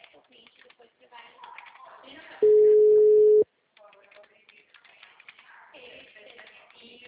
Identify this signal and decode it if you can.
Ringback tone (Europe, 425 Hz), with speech.